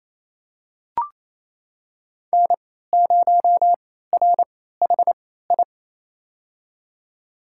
Morse code, heard as a keyed tone, about 28 wpm, spaced as for 16 wpm, 700 Hz. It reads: N0RHI
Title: USA Call Signs - 3x Character Spacing - 28wpm